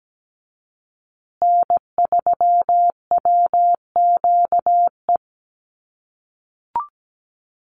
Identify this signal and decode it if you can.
Morse code, heard as a keyed tone, about 17 wpm, 700 Hz. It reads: N3WQE
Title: USA Call Signs Repeated 2x 17wpm